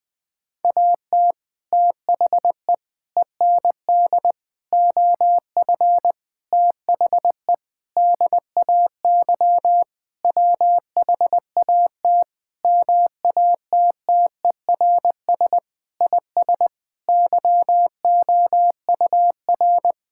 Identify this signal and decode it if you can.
Morse code, heard as a keyed tone, about 21 wpm, 700 Hz. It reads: AT THE END OF THE DAY WHAT MATTERS IS YOUR
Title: Sentences with 6-Word Phrases - Spoken Beginning - 20wpm